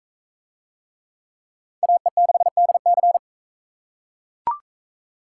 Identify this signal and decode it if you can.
Morse code, heard as a keyed tone, about 39 wpm, 700 Hz. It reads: AE6DC